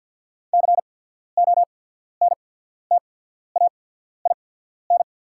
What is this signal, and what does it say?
Morse code, heard as a keyed tone, about 48 wpm, spaced as for 14 wpm, 690 Hz. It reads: CONTAIN
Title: 8x Character Spacing - Top 500 Words - 50wpm